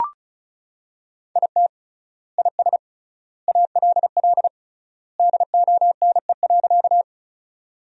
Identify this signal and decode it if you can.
Morse code, heard as a keyed tone, about 34 wpm, 700 Hz. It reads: IT IS ALL DONE.
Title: 3x Word-Spacing - Sentences from Top 500 Words - 35wpm